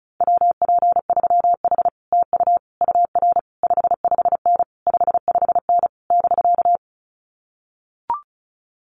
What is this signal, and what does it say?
Morse code, heard as a keyed tone, about 34 wpm, 700 Hz. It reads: WP3H TU UR 55N 55N <BK>
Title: Common QSO Phrases Encore 35wpm